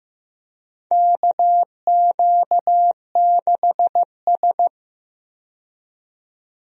Morse code, heard as a keyed tone, about 15 wpm, 700 Hz. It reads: KQ6S
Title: USA Call Signs - 4 Characters - 15wpm